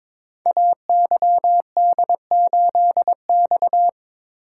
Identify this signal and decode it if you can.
Morse code, heard as a keyed tone, about 23 wpm, 700 Hz. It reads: AYD8X
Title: Code Groups - 5 Letter:Numbers - 22wpm